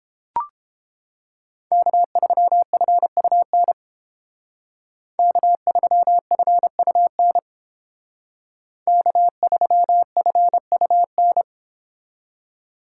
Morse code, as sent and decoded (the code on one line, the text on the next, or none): -.- ...-- ..-. ..- -. / -.- ...-- ..-. ..- -. / -.- ...-- ..-. ..- -.
K3FUN K3FUN K3FUN